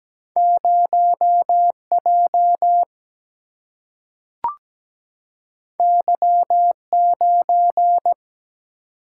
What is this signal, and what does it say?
Morse code, heard as a keyed tone, about 17 wpm, 700 Hz. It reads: 0 J Y 9
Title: Single-Letter Single-Number 17wpm